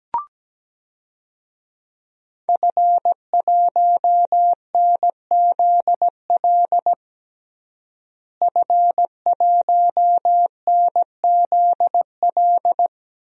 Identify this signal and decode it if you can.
Morse code, heard as a keyed tone, about 17 wpm, 700 Hz. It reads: F1NZL F1NZL
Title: DXCC-100 Call Signs - 5 Characters - Repeated 3x - 17wpm